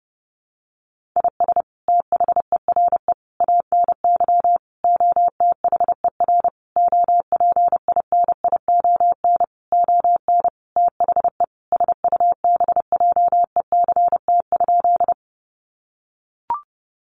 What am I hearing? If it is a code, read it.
Morse code, heard as a keyed tone, about 30 wpm, 700 Hz. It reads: IS THERE ANY OTHER OPINION ON THE SUBJECT?
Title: Sentences from Top 1200 Words 30wpm